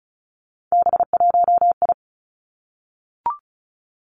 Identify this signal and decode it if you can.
Morse code, heard as a keyed tone, about 36 wpm, 700 Hz. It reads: B1I
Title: Three Challenging Letter-Number Combinations 35wpm